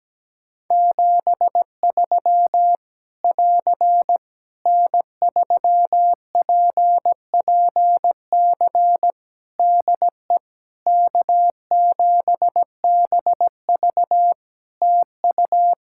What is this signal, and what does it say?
Morse code, heard as a keyed tone, about 17 wpm, 700 Hz. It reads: 73 <AR> N3PPC DE K7BV TU
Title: Practice QSO - Exchange 4 - Ending - 17wpm